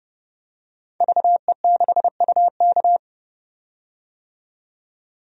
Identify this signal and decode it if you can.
Morse code, heard as a keyed tone, about 30 wpm, 700 Hz. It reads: VE6UK
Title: DXCC-100 Call Signs - 5 Characters - 30wpm